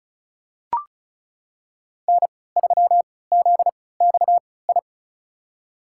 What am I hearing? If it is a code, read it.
Morse code, heard as a keyed tone, about 35 wpm, spaced as for 20 wpm, 700 Hz. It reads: N3ZXI
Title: USA Call Signs - 5 Characters - 3x Character Spacing - 35wpm